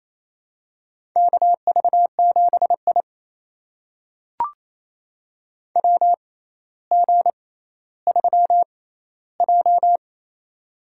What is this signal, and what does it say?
Morse code, heard as a keyed tone, about 28 wpm, spaced as for 14 wpm, 700 Hz. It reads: KV7I WG3J
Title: USA Call Signs - 4 Characters - 6x Character Spacing - 28wpm